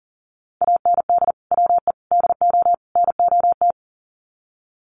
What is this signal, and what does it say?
Morse code, heard as a keyed tone, about 40 wpm, 700 Hz. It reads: AND WE DO NOT
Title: Sets of 4 Words 40wpm